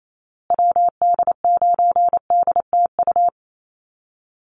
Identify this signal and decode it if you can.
Morse code, heard as a keyed tone, about 28 wpm, 700 Hz. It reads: WD9DTU
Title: USA Call Signs Repeated 3x 28wpm